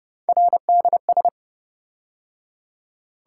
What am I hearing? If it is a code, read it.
Morse code, heard as a keyed tone, about 28 wpm, 700 Hz. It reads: RDS